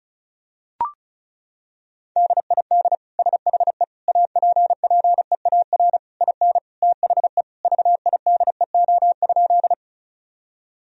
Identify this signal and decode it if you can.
Morse code, heard as a keyed tone, about 36 wpm, 700 Hz. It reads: DID SHE APPEAR IN THE VIDEO?
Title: Sentences from Top 1000 Words - Review Part 4 - 35wpm